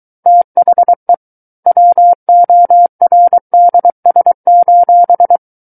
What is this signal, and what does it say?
Morse code, heard as a keyed tone, about 23 wpm, 700 Hz. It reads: THE WORDS:
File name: Book One / 14-Chapter 14 - 23 WPM